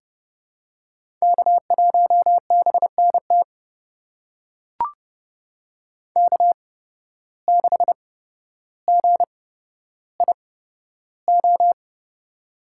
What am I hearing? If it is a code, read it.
Morse code, heard as a keyed tone, about 29 wpm, spaced as for 13 wpm, 700 Hz. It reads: K1BNT K6GIO
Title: USA Call Signs - 5 Characters - 8x Character Spacing - 30wpm